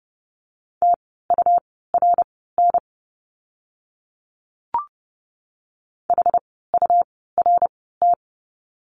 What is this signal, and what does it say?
Morse code, heard as a keyed tone, about 30 wpm, spaced as for 17 wpm, 700 Hz. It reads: TURN HURT
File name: Lesson 17f - Letter u - Words 30wpm